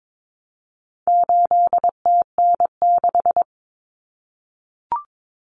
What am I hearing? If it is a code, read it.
Morse code, heard as a keyed tone, about 22 wpm, 700 Hz. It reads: 8TN6
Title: Common Cut Numbers - 4-Digit 22wpm